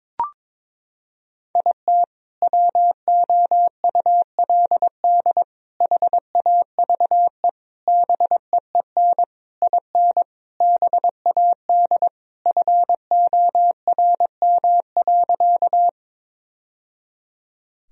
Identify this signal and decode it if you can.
Morse code, heard as a keyed tone, about 22 wpm, 700 Hz. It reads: IT WOULD HAVE BEEN IN BAD FORM.
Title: Sentences from Top 1000 Words - Review Part 1 - 22wpm